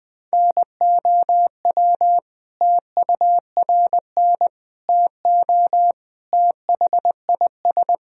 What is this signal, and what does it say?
Morse code, heard as a keyed tone, about 20 wpm, 700 Hz. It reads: NOW TURN TO THIS
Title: Sentences from Top 1000 Words - Review Part 1 - 20wpm